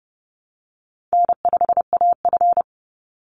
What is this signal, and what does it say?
Morse code, heard as a keyed tone, about 30 wpm, 700 Hz. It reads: N5AF
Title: Easy Call Signs Repeated 3x - 30wpm